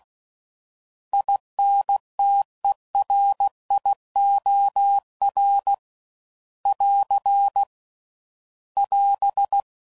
Morse code, heard as a keyed tone, about 16 wpm, 800 Hz. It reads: INTERIOR <AR> <AS>
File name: Prefixes / By First Letter / Is